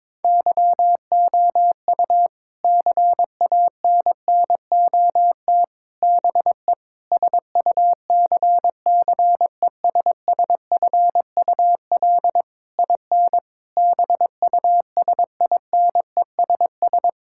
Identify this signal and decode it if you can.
Morse code, heard as a keyed tone, about 22 wpm, 700 Hz. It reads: YOU CANNOT BE SUCCESSFUL IN BUSINESS
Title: Sentences from Top 1000 Words - Review Part 6 - 22wpm